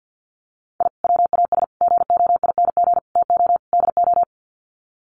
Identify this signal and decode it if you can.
Morse code, heard as a keyed tone, about 50 wpm, 700 Hz. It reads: I WAS GOING TO DO
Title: Sets of 5 Words Encore2 50wpm